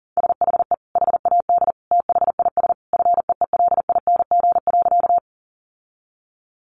Morse code, heard as a keyed tone, about 40 wpm, 700 Hz. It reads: SHE HAD THIS FEELING.
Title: Sentences from Top 1200 Words 40wpm